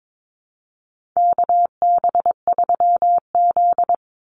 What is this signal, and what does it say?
Morse code, heard as a keyed tone, about 22 wpm, 700 Hz. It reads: KB3Z
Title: Call Signs Repeated 2x - 22wpm